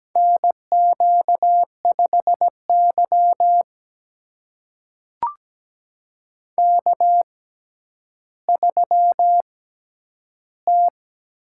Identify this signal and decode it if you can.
Morse code, heard as a keyed tone, about 17 wpm, spaced as for 10 wpm, 700 Hz. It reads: NQ5Y K3T
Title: USA Call Signs - 6x Character Spacing - 17wpm